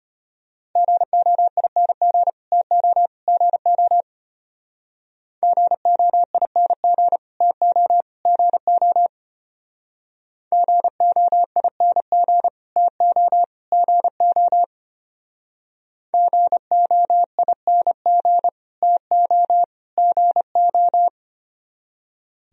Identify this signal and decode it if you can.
Morse code, heard as a keyed tone, about 32 wpm, 700 Hz. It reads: GOING TO GO GOING TO GO GOING TO GO GOING TO GO